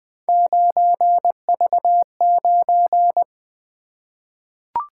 Morse code, heard as a keyed tone, about 20 wpm, 700 Hz. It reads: 9V9
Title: Three Letter-Number Combinations - 20wpm